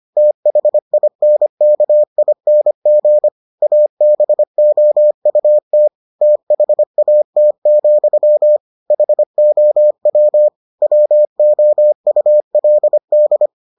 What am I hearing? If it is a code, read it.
Morse code, heard as a keyed tone, about 25 wpm, 600 Hz. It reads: THINKING ABOUT THAT, HOW WOULD